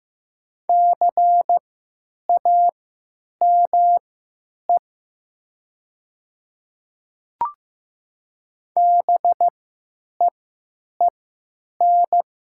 Morse code, heard as a keyed tone, about 15 wpm, spaced as for 9 wpm, 700 Hz. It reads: CAME BEEN
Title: 3x Character Spacing - Top 200 Words - 15wpm